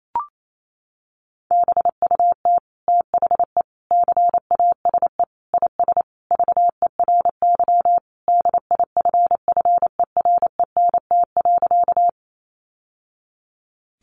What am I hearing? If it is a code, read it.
Morse code, heard as a keyed tone, about 28 wpm, 700 Hz. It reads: BUT THE CASE IS VERY DIFFERENT.
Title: Sentences from Top 700 Words 28wpm